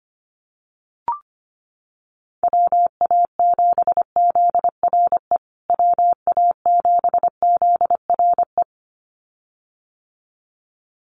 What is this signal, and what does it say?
Morse code, heard as a keyed tone, about 25 wpm, 700 Hz. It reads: WA7ZRE WA7ZRE
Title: USA Call Signs Repeated 2x 25wpm